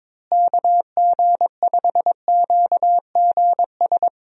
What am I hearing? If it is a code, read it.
Morse code, heard as a keyed tone, about 22 wpm, 700 Hz. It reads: KG5QGS
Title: USA Call Signs - 6 Characters - Repeated 3x - 22wpm